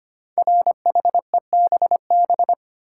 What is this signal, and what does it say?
Morse code, heard as a keyed tone, about 25 wpm, 700 Hz. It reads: RHEBB
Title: Code Groups - 5 Letters - 25wpm